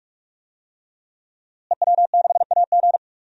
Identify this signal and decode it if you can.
Morse code, heard as a keyed tone, about 43 wpm, 700 Hz. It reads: EW6AG
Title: DXCC-100 Call Signs - 5 Characters - Repeated 3x - 45wpm